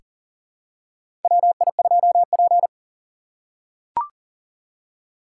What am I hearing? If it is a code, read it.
Morse code, heard as a keyed tone, about 40 wpm, 700 Hz. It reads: WI2P
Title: USA Call Signs - 4 Characters - 3x Character Spacing - 40wpm